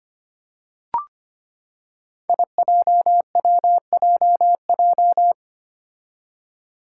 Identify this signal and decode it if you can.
Morse code, heard as a keyed tone, about 25 wpm, 700 Hz. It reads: IJWJJ
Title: Five Challenging Letter-Number Combinations 25wpm